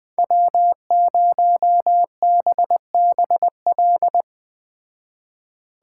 Morse code, heard as a keyed tone, about 20 wpm, 700 Hz. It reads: W0BBL
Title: Speed Racing - USA 1x3 Call Signs 17wpm